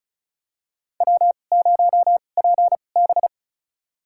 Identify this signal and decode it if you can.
Morse code, heard as a keyed tone, about 36 wpm, spaced as for 25 wpm, 700 Hz. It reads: W0PB
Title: USA Call Signs - 4 Characters - 2x Character Spacing - 35wpm